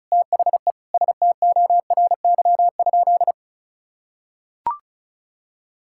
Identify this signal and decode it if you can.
Morse code, heard as a keyed tone, about 34 wpm, 700 Hz. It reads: THE STORY?
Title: Sentences from Top 1000 Words - Review Part 5 - 35wpm